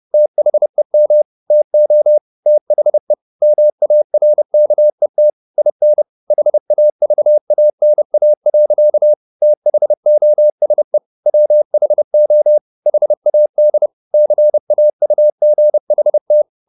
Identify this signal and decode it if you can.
Morse code, heard as a keyed tone, about 30 wpm, 600 Hz. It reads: THEM TO THE MARKET IN HAVANA. THOSE WHO HAD CAUGHT
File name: the old man and the sea / the old man and the sea 1 30wpm